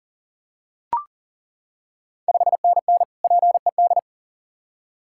Morse code, heard as a keyed tone, about 39 wpm, 690 Hz. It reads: 5NN PED
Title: Ontario QSO Party Exchange 40wpm